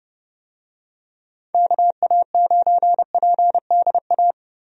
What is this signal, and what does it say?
Morse code, heard as a keyed tone, about 30 wpm, 700 Hz. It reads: KA9PDA
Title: USA Call Signs - 6 Characters - 30wpm